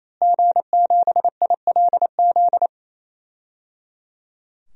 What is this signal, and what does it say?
Morse code, heard as a keyed tone, about 28 wpm, 700 Hz. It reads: G7ILZ
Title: DXCC-100 Call Signs - 5 Characters - Repeated 3x - 28wpm